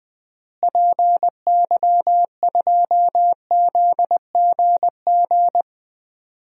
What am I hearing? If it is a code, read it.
Morse code, heard as a keyed tone, about 20 wpm, 700 Hz. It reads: PY2ZGG
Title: DXCC-100 Call Signs - 6 Characters - Repeated 2x - 20wpm